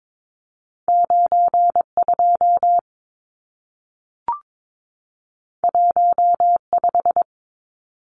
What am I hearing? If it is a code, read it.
Morse code, heard as a keyed tone, about 22 wpm, 700 Hz. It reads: 92 15
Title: Numbers - 2-Digit 22wpm